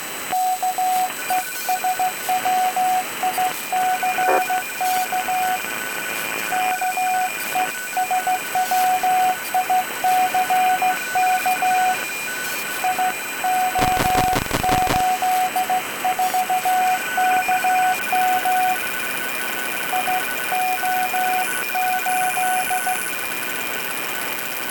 Morse code, heard as a keyed tone, about 15 wpm, 740 Hz, with radio noise. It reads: KESWICK KESWICK IO84KM IO8